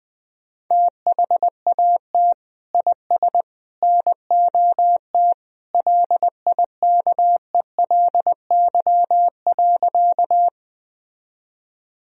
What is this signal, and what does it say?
Morse code, heard as a keyed tone, about 21 wpm, 700 Hz. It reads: THAT IS NOT LIKELY.
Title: Sentences from Top 1000 Words - Review Part 4 - 20wpm